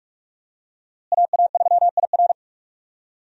Morse code, heard as a keyed tone, about 46 wpm, 700 Hz. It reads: AA3IR
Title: USA Call Signs - 5 Characters - Repeated 3x - 45wpm